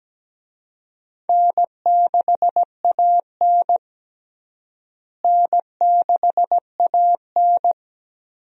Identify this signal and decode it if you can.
Morse code, heard as a keyed tone, about 17 wpm, 700 Hz. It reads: N6AN N6AN
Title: USA Call Signs - 4 Characters - Repeated 3x - 17wpm